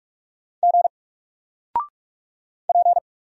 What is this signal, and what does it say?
Morse code, heard as a keyed tone, about 44 wpm, 700 Hz. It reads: G P
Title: Single Letter-Number 45wpm